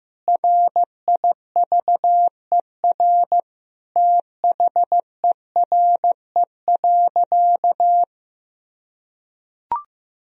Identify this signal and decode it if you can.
Morse code, heard as a keyed tone, about 16 wpm, 700 Hz. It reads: RIVER THERE.
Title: Sentences from Top 1000 Words - Review Part 3 - 15wpm